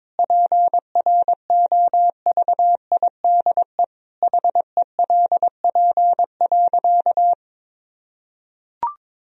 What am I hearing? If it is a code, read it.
Morse code, heard as a keyed tone, about 22 wpm, 700 Hz. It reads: PROVIDE HELP.
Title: Sentences from Top 1000 Words - Review Part 3 - 22wpm